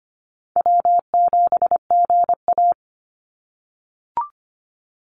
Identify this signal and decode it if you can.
Morse code, heard as a keyed tone, about 25 wpm, 700 Hz. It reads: W7GA
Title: NA SOTA Chaser Callsigns 25wpm